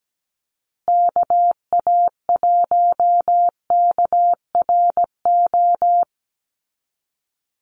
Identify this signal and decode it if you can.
Morse code, heard as a keyed tone, about 17 wpm, 700 Hz. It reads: KA1KRO
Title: USA Call Signs Repeated 2x 17wpm